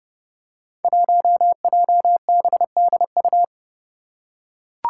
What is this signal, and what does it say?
Morse code, heard as a keyed tone, about 30 wpm, 700 Hz. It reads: 1JBDU
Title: Five Challenging Letter-Number Combinations 30wpm